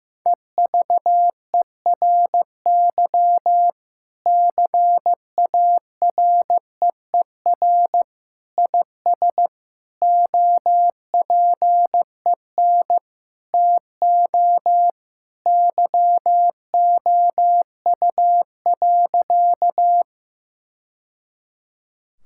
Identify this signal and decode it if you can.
Morse code, heard as a keyed tone, about 15 wpm, 700 Hz. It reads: EVERY CAREER IS OPEN TO YOU.
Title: Sentences from Top 1000 Words - Review Part 6 - 15wpm